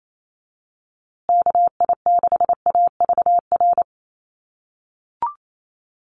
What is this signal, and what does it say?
Morse code, heard as a keyed tone, about 28 wpm, 700 Hz. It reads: KI6AVR
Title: USA Call Signs Repeated 2x 28wpm